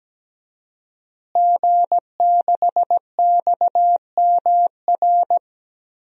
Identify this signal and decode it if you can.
Morse code, heard as a keyed tone, about 17 wpm, 700 Hz. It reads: G6XMR
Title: DXCC-100 Call Signs - 5 Characters - Repeated 2x - 17wpm